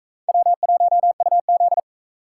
Morse code, heard as a keyed tone, about 44 wpm, 700 Hz. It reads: W1UZ